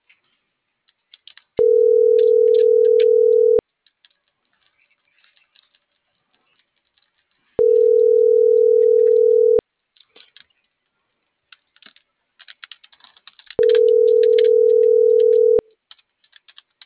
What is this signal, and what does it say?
Ringback tone (North America), 440 + 480 Hz.
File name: CENTRAL NMP / out melanie.diaznmp - .41775